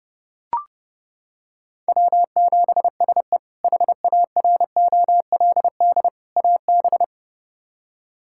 Morse code, heard as a keyed tone, about 30 wpm, 700 Hz. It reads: W7SE HAROLD AB
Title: K1USN SST QSO Practice - 30wpm